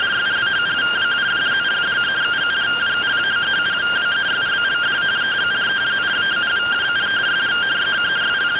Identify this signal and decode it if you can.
Two-tone teleprinter signal, about 170 Hz shift.
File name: findings / CT 205 ax mips, iDose 3 0004.gen-28 rtty rx